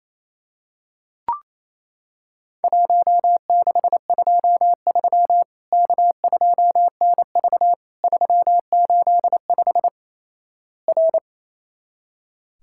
Morse code, heard as a keyed tone, about 27 wpm, 700 Hz. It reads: 1623 K2NV 385 R
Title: Worked All Europe DX Contest - QTC Exchange - 28wpm